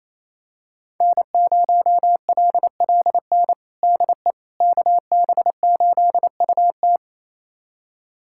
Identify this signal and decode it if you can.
Morse code, heard as a keyed tone, about 27 wpm, 700 Hz. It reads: N0LLN DE KB8UT
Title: USA Callsign DE Callsign - 5 Characters - 28wpm